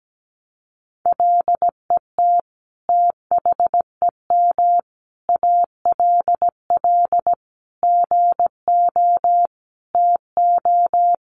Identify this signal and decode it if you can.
Morse code, heard as a keyed tone, about 17 wpm, 700 Hz. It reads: LET THEM ALL GO TO
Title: Sentences from Top 1000 Words 17wpm